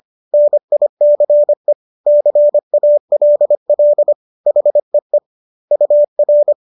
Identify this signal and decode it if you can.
Morse code, heard as a keyed tone, about 25 wpm, 590 Hz. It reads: NICE CALL HEE UR